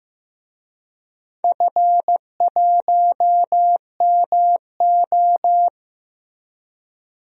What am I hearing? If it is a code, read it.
Morse code, heard as a keyed tone, about 15 wpm, 700 Hz. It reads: F1MO
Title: DXCC-100 Call Signs - 4 Characters - Repeated 3x - 15wpm